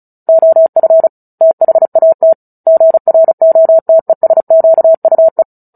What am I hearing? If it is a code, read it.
Morse code, heard as a keyed tone, about 34 wpm, 650 Hz. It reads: OF THAT GROTESQUE
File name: Book Two / 09-Chapter 9 - 35 WPM